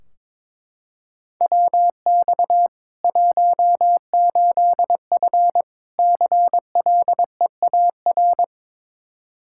Morse code, heard as a keyed tone, about 22 wpm, 700 Hz. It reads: WX 18F CLEAR
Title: Practice QSO - Exchange 2 - Info - 22wpm